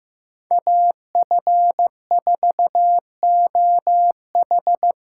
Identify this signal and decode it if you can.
Morse code, heard as a keyed tone, about 15 wpm, 700 Hz. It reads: AF4OH
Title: USA Callsign DE Callsign 15wpm